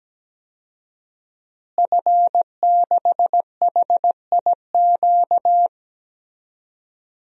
Morse code, heard as a keyed tone, about 17 wpm, 700 Hz. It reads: F6HIQ
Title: DXCC-100 Call Signs - 5 Characters - Repeated 2x - 17wpm